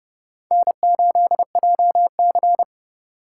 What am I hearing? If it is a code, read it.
Morse code, heard as a keyed tone, about 29 wpm, 700 Hz. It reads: N8JC